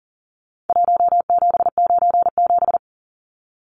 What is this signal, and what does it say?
Morse code, heard as a keyed tone, about 40 wpm, 700 Hz. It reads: 1797
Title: Numbers - Contest Style - 4-Digit 40wpm